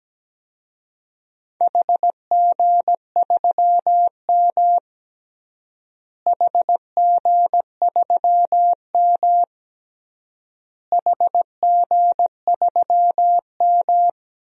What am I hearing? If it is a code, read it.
Morse code, heard as a keyed tone, about 17 wpm, 700 Hz. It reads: HG3M HG3M HG3M
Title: DXCC-100 Call Signs - 4 Characters - Repeated 3x - 17wpm